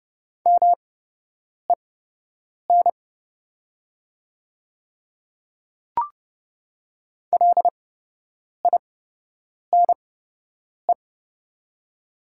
Morse code, heard as a keyed tone, about 30 wpm, spaced as for 8 wpm, 700 Hz. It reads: MEN LINE
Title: 8x Character Spacing - Top 200 Words - 30wpm